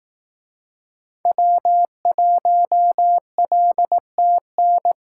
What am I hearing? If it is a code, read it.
Morse code, heard as a keyed tone, about 18 wpm, 700 Hz. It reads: W1LTN
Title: Speed Racing - USA Call Signs 15wpm